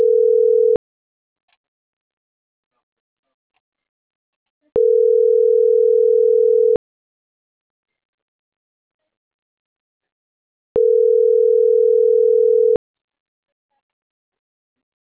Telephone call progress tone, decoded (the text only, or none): ringback tone